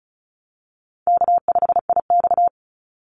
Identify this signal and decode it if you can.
Morse code, heard as a keyed tone, about 34 wpm, 700 Hz. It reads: K5IX
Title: Call Signs Repeated 3x - 35wpm